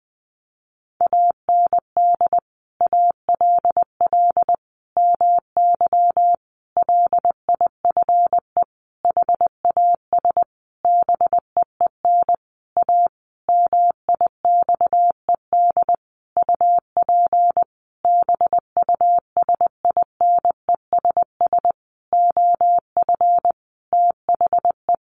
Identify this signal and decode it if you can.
Morse code, heard as a keyed tone, about 20 wpm, 700 Hz. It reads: AND ALL MY LIFE HAS BEEN A MIXED UP BUSINESS OF THE
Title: Sentences from Top 1500 Words 20wpm